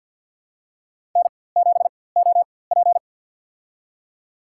Morse code, heard as a keyed tone, about 53 wpm, spaced as for 24 wpm, 700 Hz. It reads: N7OP
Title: USA Call Signs - 4x Character Spacing - 50wpm